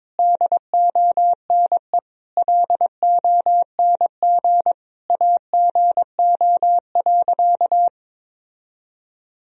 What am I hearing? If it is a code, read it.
Morse code, heard as a keyed tone, about 22 wpm, 700 Hz. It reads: DONE LONG AGO.
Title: Sentences from Top 1000 Words - Review Part 3 - 22wpm